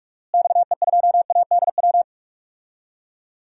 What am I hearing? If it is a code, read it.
Morse code, heard as a keyed tone, about 46 wpm, 700 Hz. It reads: XE2ADW
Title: DXCC-100 Call Signs - 6 Characters - Repeated 3x - 45wpm